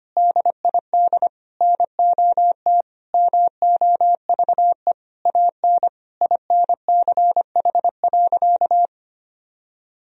Morse code, heard as a keyed tone, about 25 wpm, 700 Hz. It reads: DID NOT MOVE AN INCH.
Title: Sentences from Top 1000 Words - Review Part 3 - 25wpm